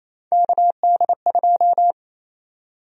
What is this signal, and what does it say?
Morse code, heard as a keyed tone, about 28 wpm, 700 Hz. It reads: KD2
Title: USA Call Sign Prefixes - 28wpm